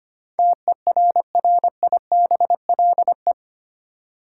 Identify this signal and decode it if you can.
Morse code, heard as a keyed tone, about 25 wpm, 700 Hz. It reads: TERRIBLE
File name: Eight Letter Words - 25wpm